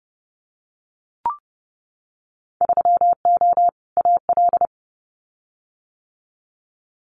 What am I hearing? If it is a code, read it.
Morse code, heard as a keyed tone, about 30 wpm, 700 Hz. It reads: 3O AL
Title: Winter Field Day Exchanges 30wpm